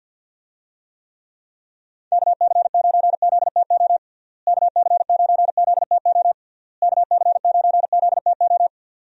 Morse code, heard as a keyed tone, about 50 wpm, 700 Hz. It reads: KC9ZTO KC9ZTO KC9ZTO
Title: USA Call Signs - 6 Characters - Repeated 3x - 50wpm